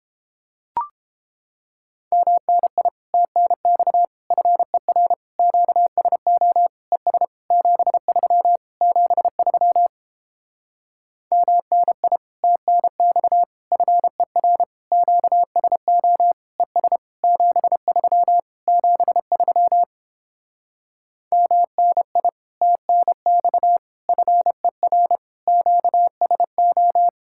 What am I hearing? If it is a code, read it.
Morse code, heard as a keyed tone, about 29 wpm, 700 Hz. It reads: MNI TNX FER QSO ES 73 73 MNI TNX FER QSO ES 73 73 MNI TNX FER QSO